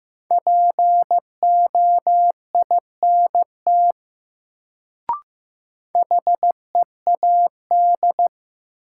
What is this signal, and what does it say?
Morse code, heard as a keyed tone, about 15 wpm, 700 Hz. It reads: POINT HEAD
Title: Sets of 3 Words - New Word Drill - 15wpm